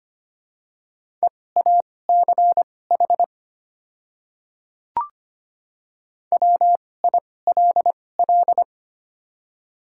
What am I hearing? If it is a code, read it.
Morse code, heard as a keyed tone, about 25 wpm, spaced as for 18 wpm, 690 Hz. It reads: EACH WILL
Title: 2x Character Spacing - Top 300 Words - 25wpm